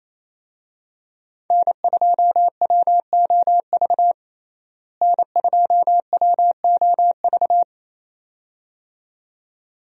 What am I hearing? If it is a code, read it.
Morse code, heard as a keyed tone, about 28 wpm, 700 Hz. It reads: N2WOV N2WOV
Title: USA Call Signs - 5 Characters - Repeated 2x - 28wpm